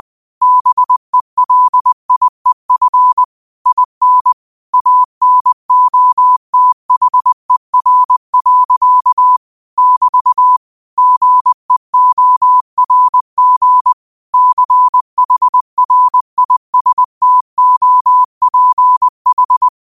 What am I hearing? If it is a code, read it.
Morse code, heard as a keyed tone, about 20 wpm, 1000 Hz. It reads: BELIEF IN ANOTHER. <BT> GEORG CHRISTOPH
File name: qotdfiles / 20wpm